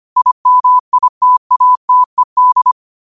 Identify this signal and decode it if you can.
Morse code, heard as a keyed tone, about 25 wpm, 1000 Hz. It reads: IMITATED
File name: qotdfiles / 25wpm